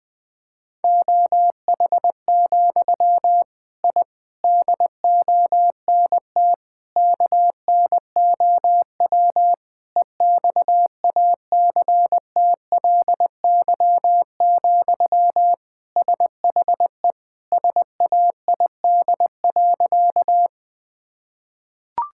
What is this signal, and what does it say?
Morse code, heard as a keyed tone, about 20 wpm, 700 Hz. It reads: OH, I DONT KNOW EXACTLY, SHE SAID.
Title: Sentences from Top 1000 Words - Review Part 6 - 20wpm